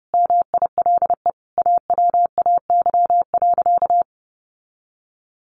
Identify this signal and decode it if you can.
Morse code, heard as a keyed tone, about 28 wpm, 700 Hz. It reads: MILE AWAY.
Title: Sentences from Top 1300 Words 30wpm